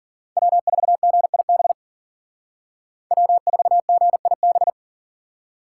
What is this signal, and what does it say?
Morse code, heard as a keyed tone, about 53 wpm, 700 Hz. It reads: W4GIB W4GIB